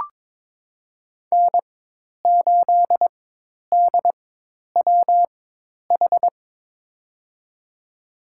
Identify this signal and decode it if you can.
Morse code, heard as a keyed tone, about 23 wpm, spaced as for 10 wpm, 700 Hz. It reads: N8DWH
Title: USA Call Signs - 4x Character Spacing - 22wpm